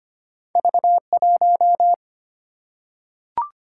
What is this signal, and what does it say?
Morse code, heard as a keyed tone, about 25 wpm, 700 Hz. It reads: V1
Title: Single-Letter Single-Number 25wpm